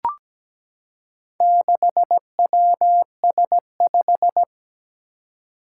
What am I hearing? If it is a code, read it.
Morse code, heard as a keyed tone, about 17 wpm, 700 Hz. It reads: 6WS5
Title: Four Challenging Letter-Number Combinations 17wpm